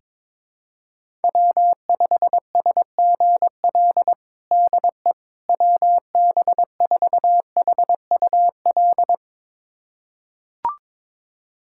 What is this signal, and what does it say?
Morse code, heard as a keyed tone, about 22 wpm, 700 Hz. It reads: W5SGL DE WB4HUL
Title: USA Callsign DE Callsign 22wpm